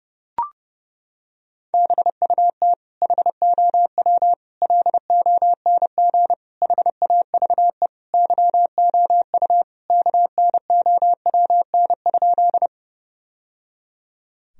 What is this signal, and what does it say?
Morse code, heard as a keyed tone, about 30 wpm, 700 Hz. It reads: BUT HOW LONG HAVE YOU KNOWN?
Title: Sentences from Top 1000 Words - Review Part 3 - 30wpm